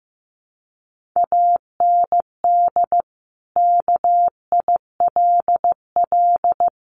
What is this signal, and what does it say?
Morse code, heard as a keyed tone, about 15 wpm, 700 Hz. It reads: AND KILL
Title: Sentences from Top 1400 Words 15wpm